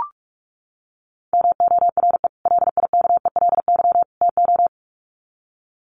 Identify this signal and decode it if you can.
Morse code, heard as a keyed tone, about 44 wpm, 700 Hz. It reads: MORE LIKELY TO
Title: Sets of 3 Words 45wpm